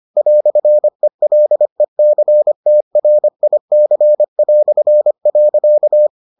Morse code, heard as a keyed tone, about 25 wpm, 600 Hz. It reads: "ELECTRIC".